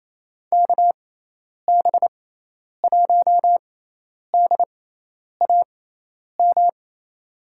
Morse code, heard as a keyed tone, about 28 wpm, spaced as for 10 wpm, 700 Hz. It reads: KB1DAM
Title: USA Call Signs - 6 Characters - 6x Character Spacing - 28wpm